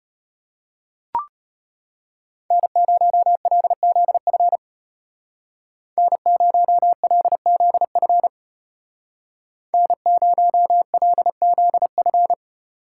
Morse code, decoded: N0LZF N0LZF N0LZF

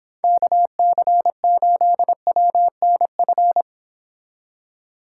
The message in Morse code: KC8WNF

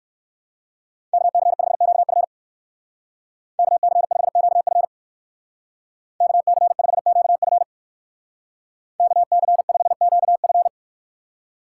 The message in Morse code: KC5QF KC5QF KC5QF KC5QF

